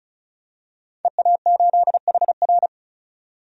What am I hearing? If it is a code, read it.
Morse code, heard as a keyed tone, about 34 wpm, 700 Hz. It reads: EA8HR
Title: DXCC-100 Call Signs - 5 Characters - 35wpm